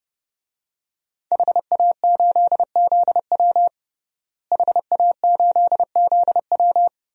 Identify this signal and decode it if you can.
Morse code, heard as a keyed tone, about 30 wpm, 700 Hz. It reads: HA8ZW HA8ZW
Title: DXCC-100 Call Signs - 5 Characters - Repeated 2x - 30wpm